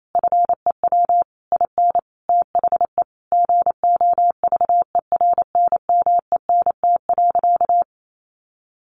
Morse code, heard as a keyed tone, about 28 wpm, 700 Hz. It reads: FEW IN THE GOVERNMENT.
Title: Sentences from Top 1300 Words 28wpm